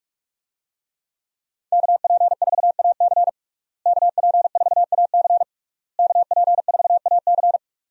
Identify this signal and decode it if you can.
Morse code, heard as a keyed tone, about 41 wpm, 700 Hz. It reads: KP4AC KP4AC KP4AC
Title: USA Call Signs - 5 Characters - Repeated 3x - 45wpm